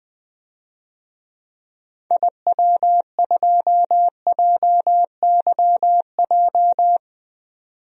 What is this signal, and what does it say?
Morse code, heard as a keyed tone, about 20 wpm, 700 Hz. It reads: IW2JYJ
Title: DXCC-100 Call Signs - 6 Characters - 20wpm